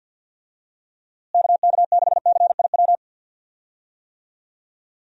Morse code, heard as a keyed tone, about 47 wpm, 700 Hz. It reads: KK6CIW
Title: USA Call Signs - 6 Characters - Repeated 2x - 50wpm